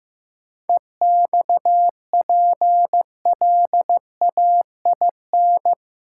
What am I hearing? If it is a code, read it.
Morse code, heard as a keyed tone, about 15 wpm, 700 Hz. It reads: EXPLAIN
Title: Sentences from Top 1100 Words - New Word Drill - 15wpm